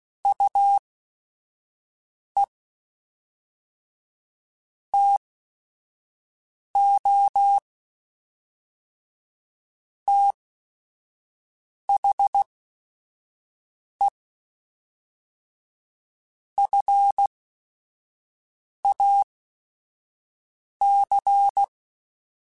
Morse code, heard as a keyed tone, about 16 wpm, spaced as for 5 wpm, 800 Hz. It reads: UE TO THE FAC
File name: Technical Subjects / Technical Subjects-mp3s / AND Gate @5